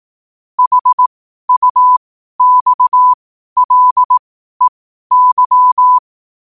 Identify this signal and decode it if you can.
Morse code, heard as a keyed tone, about 18 wpm, spaced as for 12 wpm, 1000 Hz. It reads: HUXLEY
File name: qotdfiles / 12wpm